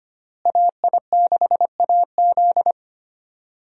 Morse code, heard as a keyed tone, about 25 wpm, 700 Hz. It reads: AI6AZ
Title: USA Call Signs - 5 Characters - Repeated 2x - 25wpm